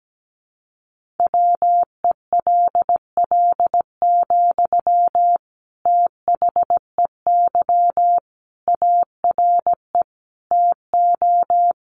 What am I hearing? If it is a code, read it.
Morse code, heard as a keyed tone, about 17 wpm, 700 Hz. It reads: WELL, THEY ARE TO
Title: Sentences from Top 1500 Words 17wpm